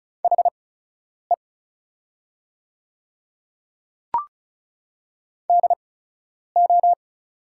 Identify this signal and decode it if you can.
Morse code, heard as a keyed tone, about 36 wpm, spaced as for 10 wpm, 700 Hz. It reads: HE DO
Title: 8x Character Spacing - Top 100 Words - 35wpm